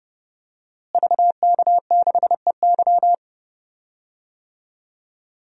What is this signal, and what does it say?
Morse code, heard as a keyed tone, about 32 wpm, 700 Hz. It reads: VK6EY
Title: DXCC-100 Call Signs - 5 Characters - Repeated 2x - 30wpm